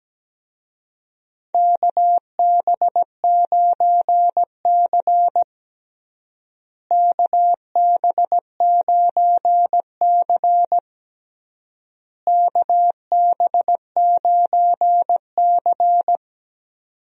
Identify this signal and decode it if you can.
Morse code, heard as a keyed tone, about 17 wpm, 700 Hz. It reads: KB9C KB9C KB9C
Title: USA Call Signs - 4 Characters - Repeated 3x - 17wpm